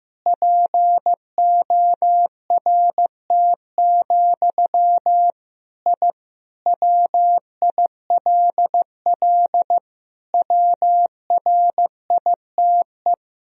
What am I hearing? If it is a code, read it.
Morse code, heard as a keyed tone, about 15 wpm, 700 Hz. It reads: PORT, I WILL WRITE
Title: Sentences from Top 1000 Words - Review Part 1 - 15wpm